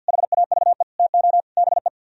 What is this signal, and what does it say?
Morse code, heard as a keyed tone, about 55 wpm, 700 Hz. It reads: HAVE TO BE
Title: Speed-Racing - Sets of 4 Words 50wpm